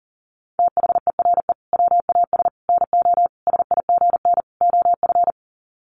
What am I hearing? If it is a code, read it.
Morse code, heard as a keyed tone, about 40 wpm, 700 Hz. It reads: THERE WAS NO SIGN OF
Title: Sets of 5 Words Encore2 40wpm